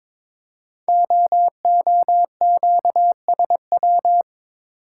Morse code, heard as a keyed tone, about 22 wpm, 700 Hz. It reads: OOQSW
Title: Code Groups - 5 Letters - 22wpm